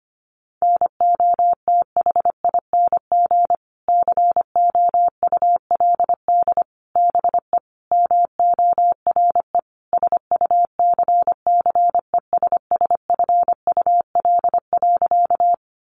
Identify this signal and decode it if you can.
Morse code, heard as a keyed tone, about 25 wpm, 700 Hz. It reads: NOTHING COULD BE MORE SUCCESSFUL.
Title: Sentences from Top 900 Words 25wpm